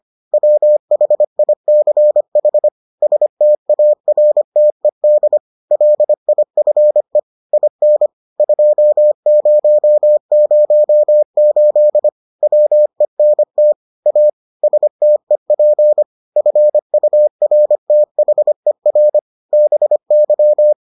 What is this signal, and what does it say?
Morse code, heard as a keyed tone, about 25 wpm, 600 Hz. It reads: WHICH STARTED LIFE IN 2008 WENT A STEP FURTHER BY